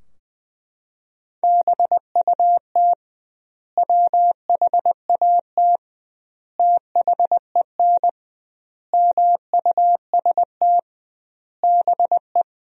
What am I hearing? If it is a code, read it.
Morse code, heard as a keyed tone, about 21 wpm, 700 Hz. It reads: BUT WHAT THEN MUST BE
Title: 2x Word-Spacing - Sentences from Top 500 Words - 20wpm